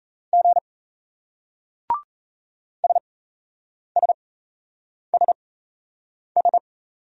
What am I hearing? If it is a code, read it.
Morse code, heard as a keyed tone, about 37 wpm, spaced as for 8 wpm, 700 Hz. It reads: G SSSS